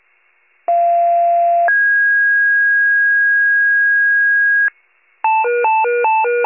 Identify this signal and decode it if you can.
Two-tone page, heard as a one-second tone, then a three-second tone, about 690 then 1650 Hz. It is heard from 0.7 to 4.7 s.
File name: wcalert